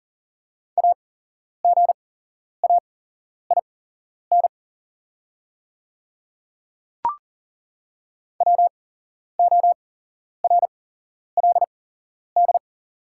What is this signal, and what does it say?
Morse code, heard as a keyed tone, about 40 wpm, spaced as for 11 wpm, 690 Hz. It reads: AGAIN WORLD
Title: 8x Character Spacing - Top 200 Words - 40wpm